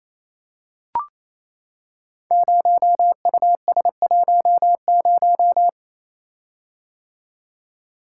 Morse code, heard as a keyed tone, about 28 wpm, 700 Hz. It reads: 0US10
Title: Code Groups - 5 Letter:Numbers - 28wpm